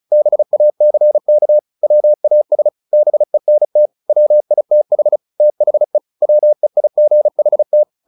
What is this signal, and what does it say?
Morse code, heard as a keyed tone, about 34 wpm, 600 Hz. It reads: BACK WAS BENT WITH THE WEIGHT
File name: the old man and the sea 2 35wpm